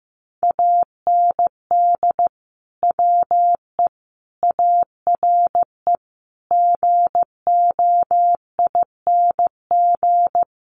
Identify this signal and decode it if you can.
Morse code, heard as a keyed tone, about 15 wpm, 700 Hz. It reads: AND WE ARE GOING
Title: Sets of 5 Words Encore2 15wpm